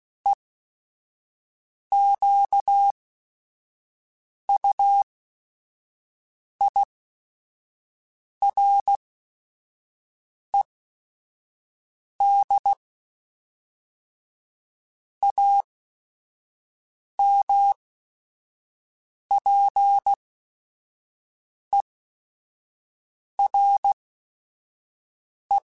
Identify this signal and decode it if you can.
Morse code, heard as a keyed tone, about 16 wpm, spaced as for 5 wpm, 800 Hz. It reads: EQUIRED AMPERE
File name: Technical Subjects / Technical Subjects-mp3s / Sw Pwr Sup @5